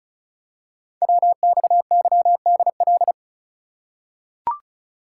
Morse code, heard as a keyed tone, about 36 wpm, 700 Hz. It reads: WXYDL
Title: Code Groups - 5 Letters - 35wpm